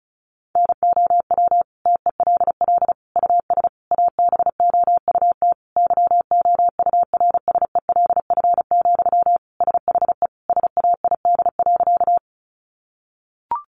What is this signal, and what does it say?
Morse code, heard as a keyed tone, about 34 wpm, 700 Hz. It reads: NOW TELL US ABOUT YOURSELF, SHE SAID.
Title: Sentences from Top 600 Words 35wpm